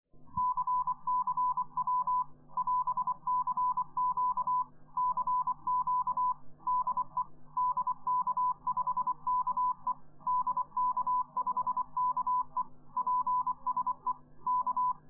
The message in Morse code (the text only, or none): CCW <AS>CQ CQ DE DK5KE DK5KE PSE K